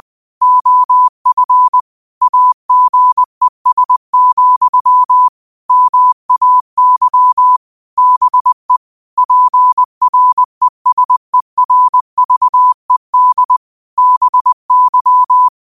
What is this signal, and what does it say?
Morse code, heard as a keyed tone, about 20 wpm, 1000 Hz. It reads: OF AGES, MAY BE PRESERVED BY